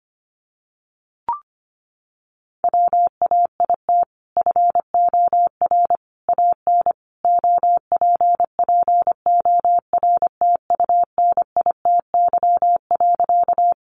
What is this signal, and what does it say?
Morse code, heard as a keyed tone, about 25 wpm, 700 Hz. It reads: WAIT FOR AN OPPORTUNITY.
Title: Sentences from Top 1000 Words 25wpm